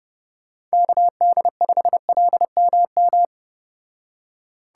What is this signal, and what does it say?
Morse code, heard as a keyed tone, about 34 wpm, 700 Hz. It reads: KD5LMM